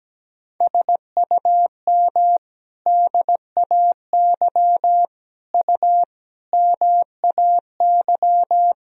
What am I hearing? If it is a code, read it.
Morse code, heard as a keyed tone, about 17 wpm, 700 Hz. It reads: SUM DAY U MAY
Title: Sentences from Top 200 Words and Abbreviations 17wpm